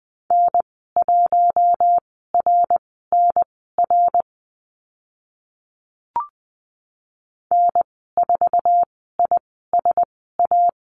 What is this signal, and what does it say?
Morse code, heard as a keyed tone, about 20 wpm, spaced as for 14 wpm, 700 Hz. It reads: N1RNR N4ISA
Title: 069 Lesson 12g - Letter d - Call Signs 20wpm